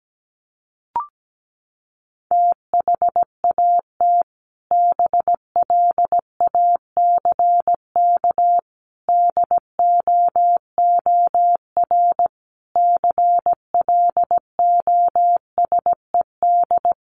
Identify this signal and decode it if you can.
Morse code, heard as a keyed tone, about 17 wpm, 700 Hz. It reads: THAT BLACK DOOR CLOSED
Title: Sentences from Top 1400 Words 17wpm